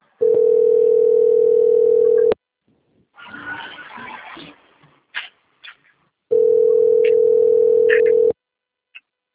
Ringback tone (North America), 440 + 480 Hz.